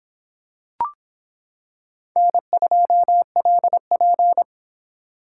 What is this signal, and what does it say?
Morse code, heard as a keyed tone, about 26 wpm, 700 Hz. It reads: N2LP